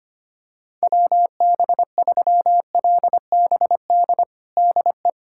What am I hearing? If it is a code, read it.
Morse code, heard as a keyed tone, about 25 wpm, 700 Hz. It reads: WB3LBD DE
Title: USA Callsign DE Callsign 25wpm